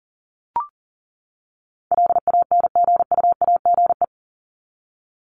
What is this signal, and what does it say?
Morse code, heard as a keyed tone, about 40 wpm, 700 Hz. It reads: LANGUAGE
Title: Top 500 Words 40wpm